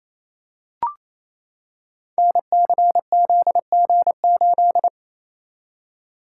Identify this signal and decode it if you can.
Morse code, heard as a keyed tone, about 27 wpm, 700 Hz. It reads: NCZG8
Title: Code Groups - 5 Letter:Numbers - 28wpm